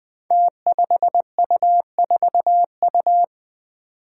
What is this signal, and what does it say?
Morse code, heard as a keyed tone, about 20 wpm, 700 Hz. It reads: T5U4U
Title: Five Challenging Letter-Number Combinations 20wpm